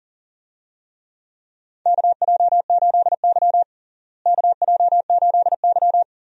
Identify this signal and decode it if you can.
Morse code, heard as a keyed tone, about 40 wpm, 700 Hz. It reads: KJ8Y KJ8Y
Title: USA Call Signs - 4 Characters - Repeated 2x - 40wpm